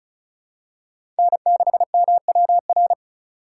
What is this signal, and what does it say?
Morse code, heard as a keyed tone, about 36 wpm, 700 Hz. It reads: N6MWR
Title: USA Call Signs - 5 Characters - 2x Character Spacing - 35wpm